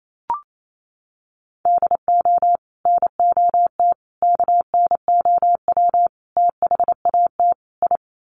Morse code, heard as a keyed tone, about 28 wpm, 700 Hz. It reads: DO NOT KNOW THAT I
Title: Sets of 5 Words Encore2 28wpm